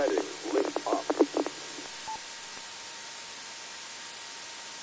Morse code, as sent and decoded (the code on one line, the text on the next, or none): none